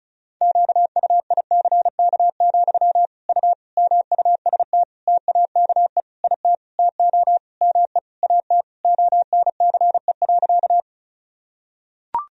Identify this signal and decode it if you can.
Morse code, heard as a keyed tone, about 36 wpm, 700 Hz. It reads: QUICK, U MUST TAKE IT TO ME AT ONCE.
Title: Sentences from Top 500 Words and Abbreviations 35wpm